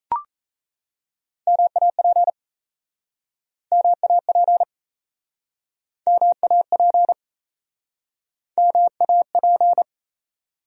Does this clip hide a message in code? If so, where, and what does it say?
Yes: Morse code from 1.5 to 9.8 s, heard as a keyed tone, about 37 wpm, 700 Hz. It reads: MAP MAP MAP MAP